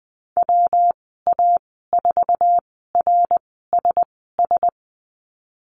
Morse code, heard as a keyed tone, about 20 wpm, spaced as for 14 wpm, 700 Hz. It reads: WA4RSS
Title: 161 Lesson 26g - Letter g - Call Signs 20wpm